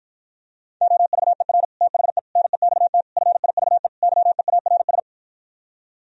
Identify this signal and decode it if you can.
Morse code, heard as a keyed tone, about 53 wpm, 700 Hz. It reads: OVER THE NEXT FIVE YEARS